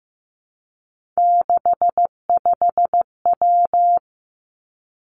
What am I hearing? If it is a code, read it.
Morse code, heard as a keyed tone, about 15 wpm, 700 Hz. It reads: 65W
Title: Three Challenging Letter-Number Combinations 15wpm